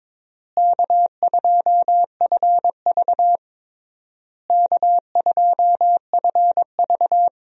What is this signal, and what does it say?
Morse code, heard as a keyed tone, about 22 wpm, 700 Hz. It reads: K2FV K2FV
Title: USA Call Signs - 4 Characters - Repeated 3x - 22wpm